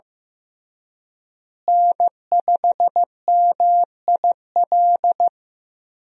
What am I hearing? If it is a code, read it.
Morse code, heard as a keyed tone, about 15 wpm, 700 Hz. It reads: N5MIL
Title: USA Call Signs - 5 Characters - Repeated 3x - 15wpm